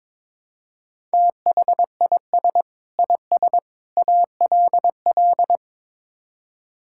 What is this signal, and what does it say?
Morse code, heard as a keyed tone, about 22 wpm, 700 Hz. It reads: THIS IS ALL
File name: practice / 22wpm / Build A Phrase - 5-Word Phrases - 22wpm